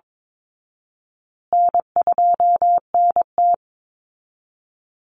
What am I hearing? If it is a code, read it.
Morse code, heard as a keyed tone, about 22 wpm, 700 Hz. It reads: N2NT
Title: Easy Call Signs Repeated 2x - 22wpm